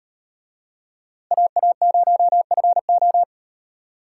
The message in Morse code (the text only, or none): AA0FO